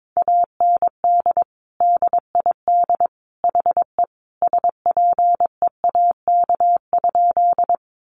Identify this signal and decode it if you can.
Morse code, heard as a keyed tone, about 22 wpm, 700 Hz. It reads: AND DID HE SPEAK?
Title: Sentences from Top 1100 Words 22wpm